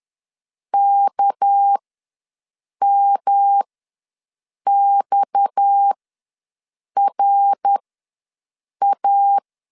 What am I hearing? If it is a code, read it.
Morse code, heard as a keyed tone, about 10 wpm, spaced as for 6 wpm, 800 Hz. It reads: KMXRA
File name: LEZIONI / 6 PKX0 / 6 PKX ws 10wpm 3706